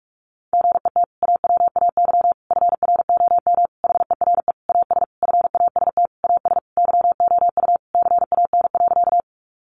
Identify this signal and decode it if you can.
Morse code, heard as a keyed tone, about 44 wpm, 700 Hz. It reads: GET AWAY FROM HERE AS FAST AS YOU CAN.
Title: Sentences from Top 600 Words 45wpm